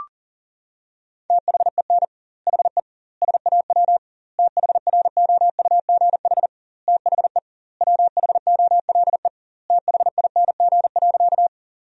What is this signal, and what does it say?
Morse code, heard as a keyed tone, about 40 wpm, 700 Hz. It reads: THEN HE SAW THROUGH THE WHOLE THING.
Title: 2x Word-Spacing - Sentences from Top 400 Words - 40wpm